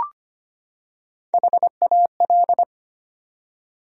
Morse code, heard as a keyed tone, about 25 wpm, 700 Hz. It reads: HAL
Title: USA Common Names - Max 5 Letters - 25wpm